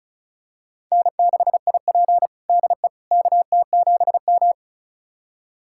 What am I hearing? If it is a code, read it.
Morse code, heard as a keyed tone, about 34 wpm, 700 Hz. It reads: N6IP DE KT7M
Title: USA Callsign DE Callsign - 4 Characters - 35wpm